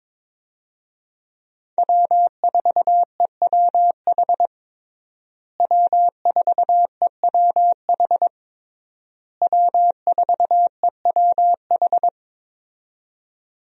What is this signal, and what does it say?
Morse code, heard as a keyed tone, about 22 wpm, 700 Hz. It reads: W4EWH W4EWH W4EWH
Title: USA Call Signs - 5 Characters - Repeated 3x - 22wpm